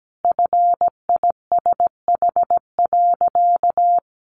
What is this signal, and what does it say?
Morse code, heard as a keyed tone, about 17 wpm, 700 Hz. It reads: FISH.
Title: Easy Phrases 17wpm